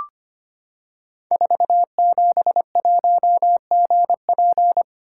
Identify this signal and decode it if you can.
Morse code, heard as a keyed tone, about 25 wpm, 700 Hz. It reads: 471GP
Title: Code Groups - 5 Letter:Numbers - 25wpm